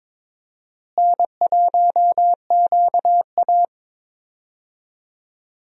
Morse code, heard as a keyed tone, about 22 wpm, 700 Hz. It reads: N1QA
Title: USA Call Signs - 4 Characters - Repeated 2x - 22wpm